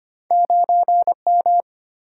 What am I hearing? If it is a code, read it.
Morse code, heard as a keyed tone, about 25 wpm, 700 Hz. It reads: 9 M
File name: Two Letter-Number Combinations - 25wpm